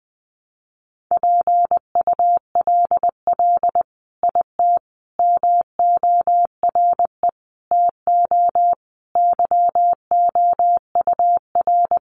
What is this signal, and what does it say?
Morse code, heard as a keyed tone, about 20 wpm, 700 Hz. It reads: PULL IT MORE TO YOUR
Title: Sentences from Top 400 Words 20wpm